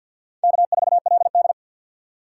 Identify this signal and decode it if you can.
Morse code, heard as a keyed tone, about 51 wpm, 700 Hz. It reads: K4LD
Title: USA Call Signs - 4 Characters - 3x Character Spacing - 50wpm